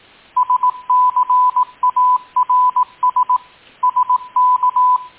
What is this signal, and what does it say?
Morse code, heard as a keyed tone, about 18 wpm, 1000 Hz. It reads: SCARS SK
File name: tin-upgrade / QSO 2020 10 15 .mp3